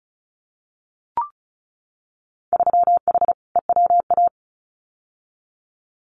Morse code, heard as a keyed tone, about 36 wpm, 700 Hz. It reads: 3H EWA
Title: Winter Field Day Exchanges Encore 35wpm